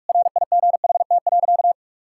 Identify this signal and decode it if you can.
Morse code, heard as a keyed tone, about 46 wpm, 700 Hz. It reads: RIGHT.
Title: Sentences from Top 1000 Words - Review Part 5 - 45wpm